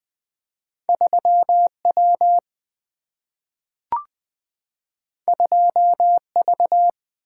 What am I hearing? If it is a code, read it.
Morse code, heard as a keyed tone, about 20 wpm, 700 Hz. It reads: 3 W 2 V
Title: Single-Letter Single-Number 20wpm